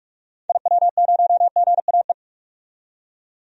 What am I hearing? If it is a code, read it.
Morse code, heard as a keyed tone, about 47 wpm, 700 Hz. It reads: IW0GAE
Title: DXCC-100 Call Signs - 6 Characters - 45wpm